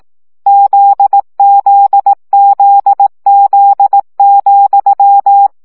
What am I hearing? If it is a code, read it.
Morse code, heard as a keyed tone, about 18 wpm, 800 Hz. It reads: ZZZZ,
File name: CW gyakorlas z 7 vesszo 15 baud